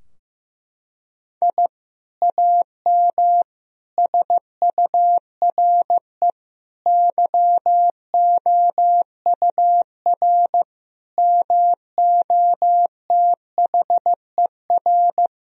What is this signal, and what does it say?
Morse code, heard as a keyed tone, about 16 wpm, 700 Hz. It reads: I AM SURE YOUR MOTHER
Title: Sentences from Top 1000 Words - Review Part 3 - 15wpm